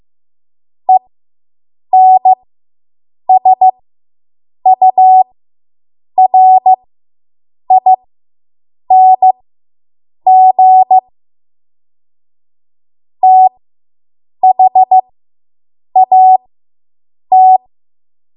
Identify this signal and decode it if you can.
Morse code, heard as a keyed tone, about 15 wpm, spaced as for 7 wpm, 750 Hz. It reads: ENSURING THAT